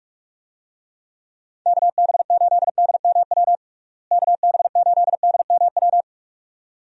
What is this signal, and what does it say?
Morse code, heard as a keyed tone, about 41 wpm, 700 Hz. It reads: KB8DMW KB8DMW
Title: USA Call Signs - 6 Characters - Repeated 2x - 45wpm